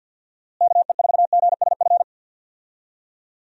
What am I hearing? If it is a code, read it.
Morse code, heard as a keyed tone, about 50 wpm, 700 Hz. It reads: KE4GSF
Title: USA Call Signs - 6 Characters - Repeated 2x - 50wpm